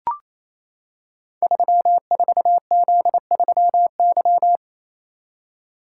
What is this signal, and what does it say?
Morse code, heard as a keyed tone, about 28 wpm, 700 Hz. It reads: 34Z3Y
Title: Code Groups - 5 Letter:Numbers - 28wpm